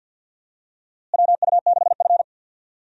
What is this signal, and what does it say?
Morse code, heard as a keyed tone, about 48 wpm, 700 Hz. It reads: WU6F
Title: USA Call Signs - 4 Characters - Repeated 3x - 50wpm